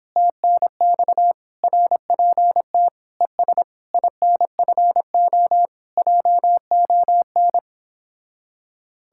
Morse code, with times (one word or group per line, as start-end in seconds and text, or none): 0.2-1.3 TNX
1.6-2.9 RPT
3.2-3.6 ES
3.9-5.7 INFO
6.0-7.6 JON